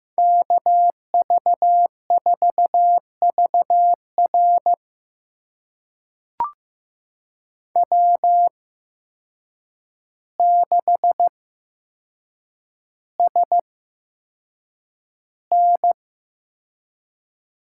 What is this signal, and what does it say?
Morse code, heard as a keyed tone, about 15 wpm, spaced as for 7 wpm, 700 Hz. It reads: KV4VR W6SN
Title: USA Call Signs - 8x Character Spacing - 15wpm